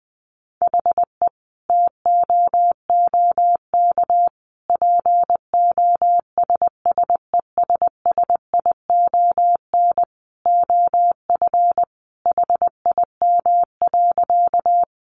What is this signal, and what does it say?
Morse code, heard as a keyed tone, about 20 wpm, 700 Hz. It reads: HE TOOK POSSESSION OF HIM.
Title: Sentences from Top 1400 Words 20wpm